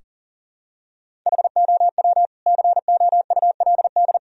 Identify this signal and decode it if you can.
Morse code, heard as a keyed tone, about 40 wpm, 700 Hz. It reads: HOW COULD
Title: Sentences from Top 1000 Words - Review Part 3 - 40wpm